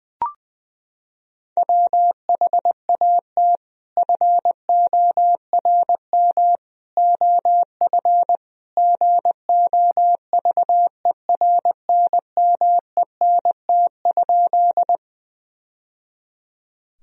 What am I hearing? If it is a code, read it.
Morse code, heard as a keyed tone, about 20 wpm, 700 Hz. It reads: WHAT FORM OF GOVERNMENT?
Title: Sentences from Top 1000 Words - Review Part 3 - 20wpm